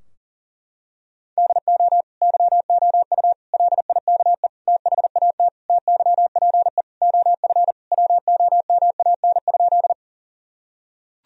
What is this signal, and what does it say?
Morse code, heard as a keyed tone, about 40 wpm, 700 Hz. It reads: DO YOU LIKE THAT TYPE OF WOMAN?
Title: Sentences from Top 1000 Words - Review Part 3 - 40wpm